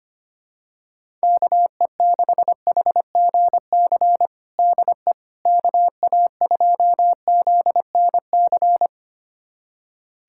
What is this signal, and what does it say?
Morse code, heard as a keyed tone, about 25 wpm, 700 Hz. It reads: KE6HGC DE KA2ZNC
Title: USA Callsign DE Callsign - 6 Characters - 25wpm